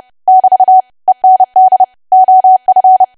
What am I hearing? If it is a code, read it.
Morse code, heard as a keyed tone, about 30 wpm, 750 Hz. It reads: <BT> END OF
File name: cwa / W1AW-Level3 / 30WPM